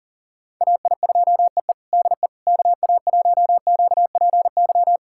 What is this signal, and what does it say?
Morse code, heard as a keyed tone, about 40 wpm, 700 Hz. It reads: AI2EE DE KA1QPY
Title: USA Callsign DE Callsign 40wpm